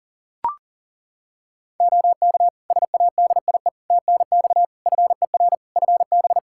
Morse code, heard as a keyed tone, about 40 wpm, 700 Hz. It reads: OK SADIE TNX FER FB
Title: Practice QSO - Exchange 4 - Ending - 40wpm